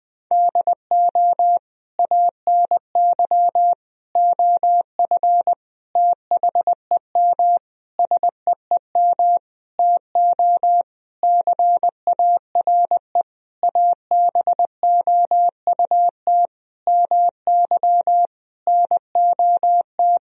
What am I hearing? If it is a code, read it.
Morse code, heard as a keyed tone, about 20 wpm, 700 Hz. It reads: DO ANY OF THEM SEEM TO CARE ABOUT MY NOT
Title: Sentences from Top 1000 Words - Review Part 3 - 20wpm